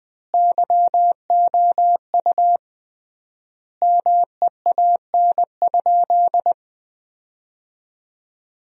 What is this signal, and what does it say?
Morse code, heard as a keyed tone, about 21 wpm, 700 Hz. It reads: YOU MEAN?
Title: 3x Word-Spacing - Sentences from Top 200 Words - 20wpm